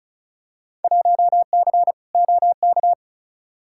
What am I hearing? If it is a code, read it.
Morse code, heard as a keyed tone, about 34 wpm, 700 Hz. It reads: 1C OK
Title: Field Day Exchanges Encore - 35wpm